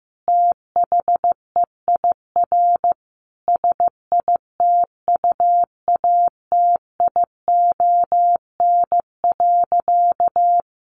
Morse code, heard as a keyed tone, about 15 wpm, 700 Hz. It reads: THEIR SITUATION.
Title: Sentences from Top 1200 Words 15wpm